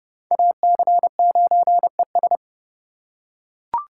Morse code, heard as a keyed tone, about 30 wpm, 700 Hz. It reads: AC9ES
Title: USA Call Signs - 5 Characters - 2x Character Spacing - 30wpm